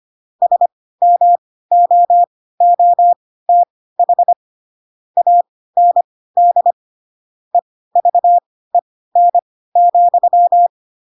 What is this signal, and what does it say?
Morse code, heard as a keyed tone, about 25 wpm, spaced as for 16 wpm, 700 Hz. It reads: SMOOTH AND EVEN,
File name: Pinocchio / Pinocchio ch1 10 25 ik1hge